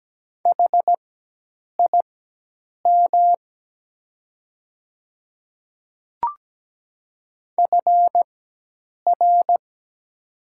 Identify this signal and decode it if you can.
Morse code, heard as a keyed tone, about 17 wpm, spaced as for 8 wpm, 700 Hz. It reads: HIM FR
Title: 4x Character Spacing - Top 300 Words - 17wpm